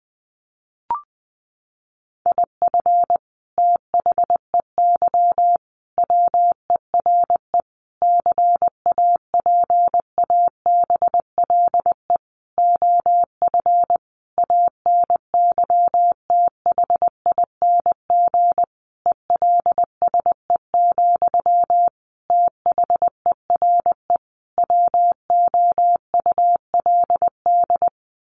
Morse code, heard as a keyed tone, about 20 wpm, 700 Hz. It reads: IF THEY WERE CAPABLE OF ANYTHING ELSE, THERE WOULD
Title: Sentences from Top 1200 Words 20wpm